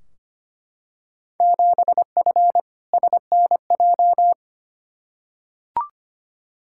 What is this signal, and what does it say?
Morse code, heard as a keyed tone, about 25 wpm, 700 Hz. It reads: 7F SNJ
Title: Field Day Exchanges Encore - 25wpm